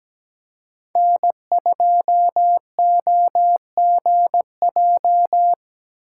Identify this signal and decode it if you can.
Morse code, heard as a keyed tone, about 17 wpm, 700 Hz. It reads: N2OGJ
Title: USA Call Signs - 5 Characters - 6x Character Spacing - 17wpm